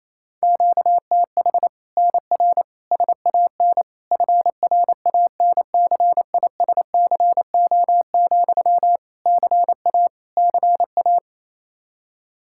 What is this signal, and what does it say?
Morse code, heard as a keyed tone, about 28 wpm, 700 Hz. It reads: QTH NR SAN FRANCISCO, CA CA
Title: Practice QSO - Exchange 1 - Beginning - 28wpm